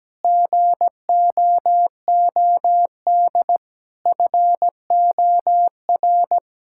Morse code, heard as a keyed tone, about 17 wpm, 700 Hz. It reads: GOOD FOR
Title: Sentences from Top 1000 Words - Review Part 6 - 17wpm